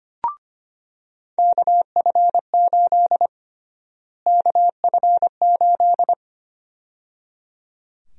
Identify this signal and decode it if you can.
Morse code, heard as a keyed tone, about 25 wpm, 700 Hz. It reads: KF8 KF8
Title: USA Call Sign Prefixes - Repeated 2x - 25wpm